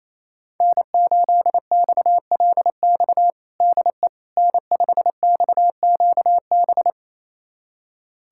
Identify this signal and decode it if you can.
Morse code, heard as a keyed tone, about 28 wpm, 700 Hz. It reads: N8XLX DE N5XQB
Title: USA Callsign DE Callsign - 5 Characters - 28wpm